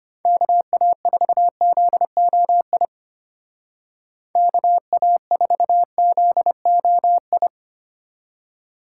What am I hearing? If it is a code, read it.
Morse code, heard as a keyed tone, about 33 wpm, 700 Hz. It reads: KA4ZOI KA4ZOI